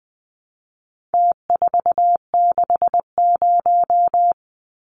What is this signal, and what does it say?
Morse code, heard as a keyed tone, about 20 wpm, 700 Hz. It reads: T460
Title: Numbers - Contest Style - 4-Digit 20wpm